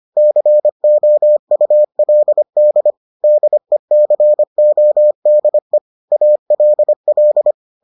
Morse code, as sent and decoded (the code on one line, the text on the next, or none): -.-. --- ..- .-.. -.. / -.. -.-. --- -.. / .- .-.. .-..
COULD DECODE ALL